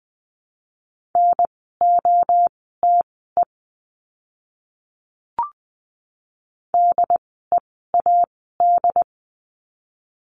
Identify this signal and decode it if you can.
Morse code, heard as a keyed tone, about 20 wpm, spaced as for 15 wpm, 700 Hz. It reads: NOTE DEAD
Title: 075 Lesson 13f - Letter L - Words 20wpm